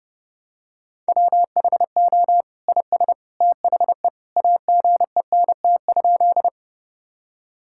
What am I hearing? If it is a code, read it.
Morse code, heard as a keyed tone, about 32 wpm, 700 Hz. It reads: WHO IS THE AGENT?
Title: Sentences from Top 1600 Words 30wpm